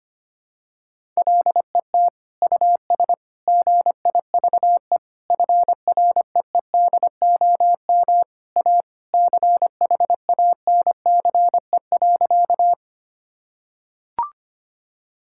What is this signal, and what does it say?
Morse code, heard as a keyed tone, about 25 wpm, 700 Hz. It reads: LET US GIVE FREEDOM A CHANCE.
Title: Sentences from Top 1000 Words - Review Part 5 - 25wpm